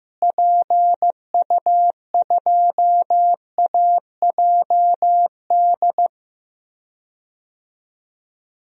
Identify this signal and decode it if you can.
Morse code, heard as a keyed tone, about 15 wpm, 700 Hz. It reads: PU2AJD
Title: DXCC-100 Call Signs - 6 Characters - Repeated 3x - 15wpm